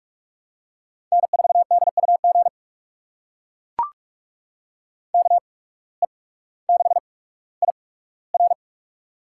Morse code, heard as a keyed tone, about 44 wpm, spaced as for 19 wpm, 700 Hz. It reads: N4DUG KE6IR
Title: USA Call Signs - 5 Characters - 8x Character Spacing - 45wpm